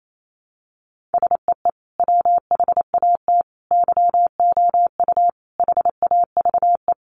Morse code, heard as a keyed tone, about 28 wpm, 700 Hz. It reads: SEE WHAT YOU HAVE
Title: Sentences from Top 500 Words 28wpm